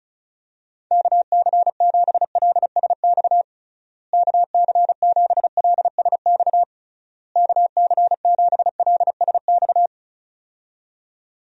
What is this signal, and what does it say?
Morse code, heard as a keyed tone, about 36 wpm, 700 Hz. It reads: KC7LSX KC7LSX KC7LSX
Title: USA Call Signs - 6 Characters - Repeated 3x - 35wpm